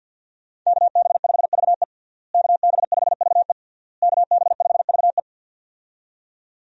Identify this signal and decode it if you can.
Morse code, heard as a keyed tone, about 53 wpm, 690 Hz. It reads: KB5VE KB5VE KB5VE
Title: USA Call Signs - 5 Characters - Repeated 3x - 50wpm